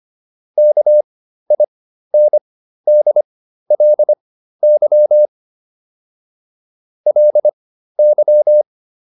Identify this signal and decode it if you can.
Morse code, heard as a keyed tone, about 25 wpm, spaced as for 13 wpm, 600 Hz. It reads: KINDLY LY